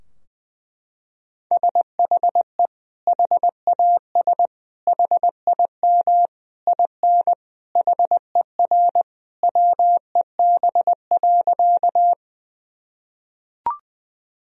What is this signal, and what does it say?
Morse code, heard as a keyed tone, about 20 wpm, 700 Hz. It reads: SHE HAS HIM IN HER WEB.
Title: Sentences from Top 1000 Words - Review Part 4 - 20wpm